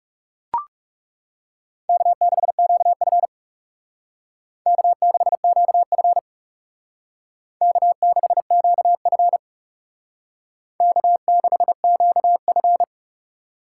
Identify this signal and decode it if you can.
Morse code, heard as a keyed tone, about 36 wpm, 700 Hz. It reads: K6QF K6QF K6QF K6QF
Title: Speed Racing - USA 1x2 Call Signs 30wpm